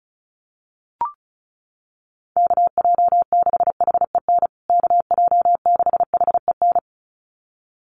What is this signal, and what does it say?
Morse code, heard as a keyed tone, about 34 wpm, 700 Hz. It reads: KJ6HEN KJ6HEN
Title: USA Call Signs Repeated 2x 35wpm